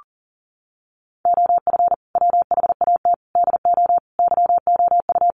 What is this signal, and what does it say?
Morse code, heard as a keyed tone, about 40 wpm, 700 Hz. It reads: OF WHAT DO YOU
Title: Sentences from Top 1300 Words 40wpm